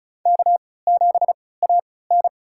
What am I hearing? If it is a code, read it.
Morse code, heard as a keyed tone, about 36 wpm, spaced as for 20 wpm, 690 Hz. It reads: K7AN
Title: USA Call Signs - 4 Characters - 3x Character Spacing - 35wpm